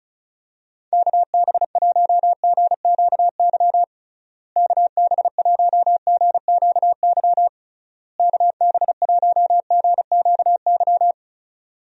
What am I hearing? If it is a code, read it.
Morse code, heard as a keyed tone, about 36 wpm, 700 Hz. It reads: KB1GQY KB1GQY KB1GQY
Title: USA Call Signs - 6 Characters - Repeated 3x - 35wpm